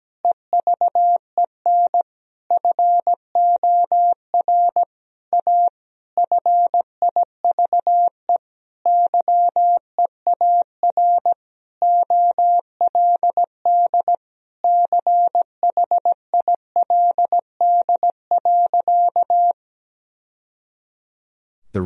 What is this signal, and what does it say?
Morse code, heard as a keyed tone, about 17 wpm, 700 Hz. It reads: EVEN FOR A FIVE YEAR OLD CHILD.
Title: Sentences from Top 1000 Words - Review Part 2 - 17wpm